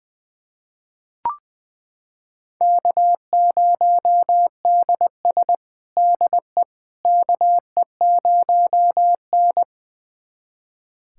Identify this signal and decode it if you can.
Morse code, heard as a keyed tone, about 20 wpm, 700 Hz. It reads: K0DS DE KE0N
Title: USA Callsign DE Callsign - 4 Characters - 20wpm